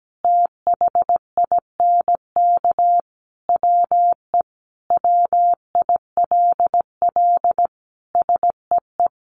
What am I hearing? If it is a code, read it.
Morse code, heard as a keyed tone, about 17 wpm, 700 Hz. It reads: THINK WE WILL SEE
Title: Sentences from Top 1200 Words 17wpm